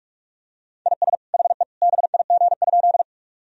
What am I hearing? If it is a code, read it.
Morse code, heard as a keyed tone, about 46 wpm, 700 Hz. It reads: IS HE BIG?
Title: Sentences from Top 1000 Words - Review Part 1 - 45wpm